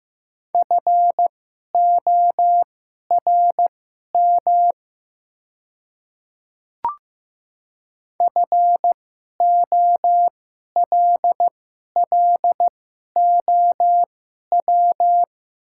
Morse code, heard as a keyed tone, about 15 wpm, spaced as for 11 wpm, 700 Hz. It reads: FORM FOLLOW
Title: 2x Character Spacing - Top 200 Words - 15wpm